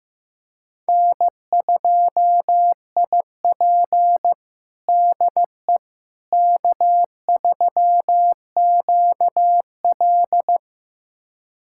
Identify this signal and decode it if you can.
Morse code, heard as a keyed tone, about 15 wpm, 700 Hz. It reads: N2IP DE K3QL
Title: USA Callsign DE Callsign - 4 Characters - 15wpm